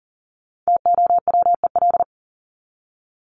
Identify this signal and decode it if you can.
Morse code, heard as a keyed tone, about 40 wpm, 700 Hz. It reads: TOWEL
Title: Top 1501-1750 Words 40wpm